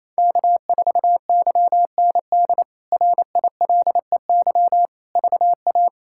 Morse code, heard as a keyed tone, about 28 wpm, 700 Hz. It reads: K4YND RILEY VA
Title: K1USN SST QSO Practice - 28wpm